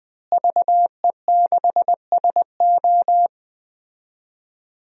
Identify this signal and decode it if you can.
Morse code, heard as a keyed tone, about 21 wpm, 700 Hz. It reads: VE6SO
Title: DXCC-100 Call Signs - 5 Characters - Repeated 3x - 20wpm